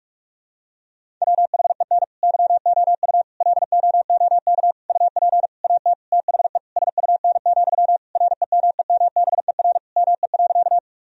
Morse code, heard as a keyed tone, about 41 wpm, 700 Hz. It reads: WHEN YOU LOOK UP AT THE SUN, REMEMBER ME.
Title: Sentences from Top 1000 Words - Review Part 1 - 45wpm